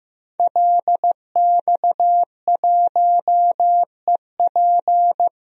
Morse code, heard as a keyed tone, about 15 wpm, 700 Hz. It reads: LX1EP
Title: DXCC-100 Call Signs - 5 Characters - 15wpm